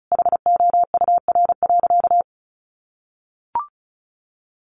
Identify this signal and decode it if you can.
Morse code, heard as a keyed tone, about 34 wpm, 700 Hz. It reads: HOUR.
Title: Sentences from Top 1500 Words 35wpm